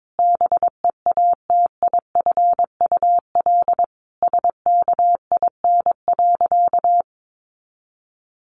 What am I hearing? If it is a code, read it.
Morse code, heard as a keyed tone, about 22 wpm, 700 Hz. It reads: BEATIFUL SKIN.
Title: Sentences from Top 1300 Words 22wpm